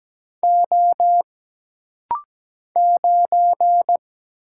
Morse code, heard as a keyed tone, about 17 wpm, 700 Hz. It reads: O 9
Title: Single Letter-Number - Rapid-Fire 17wpm